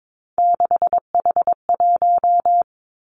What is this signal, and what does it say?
Morse code, heard as a keyed tone, about 22 wpm, 700 Hz. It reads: 6H1
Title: Three Challenging Letter-Number Combinations 22wpm